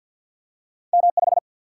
Morse code, heard as a keyed tone, about 53 wpm, 700 Hz. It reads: M 5
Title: Single-Letter Single-Number 50wpm